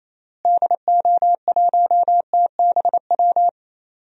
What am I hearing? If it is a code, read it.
Morse code, heard as a keyed tone, about 28 wpm, 700 Hz. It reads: DO1TBW
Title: DXCC-100 Call Signs - 6 Characters - Repeated 2x - 28wpm